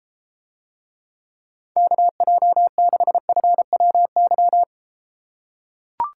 Morse code, heard as a keyed tone, about 32 wpm, 700 Hz. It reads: KJ6FWY